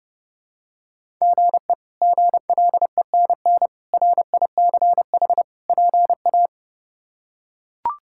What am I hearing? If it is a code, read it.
Morse code, heard as a keyed tone, about 30 wpm, 700 Hz. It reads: GE GLENN RICH PA
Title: K1USN SST QSO Practice - 30wpm